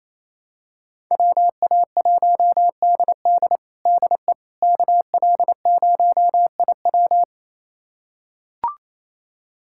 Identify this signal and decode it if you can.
Morse code, heard as a keyed tone, about 28 wpm, 700 Hz. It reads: WA1DD DE KL0IW
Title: USA Callsign DE Callsign - 5 Characters - 28wpm